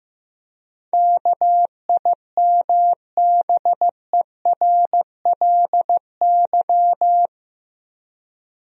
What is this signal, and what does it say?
Morse code, heard as a keyed tone, about 15 wpm, 700 Hz. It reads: KIMBERLY
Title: USA Common Names - 15wpm